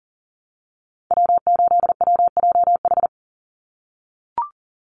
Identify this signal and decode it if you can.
Morse code, heard as a keyed tone, about 40 wpm, 700 Hz. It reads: W8WJH
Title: USA Call Signs 40wpm